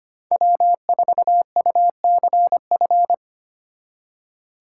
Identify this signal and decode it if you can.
Morse code, heard as a keyed tone, about 33 wpm, 700 Hz. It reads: W4UCF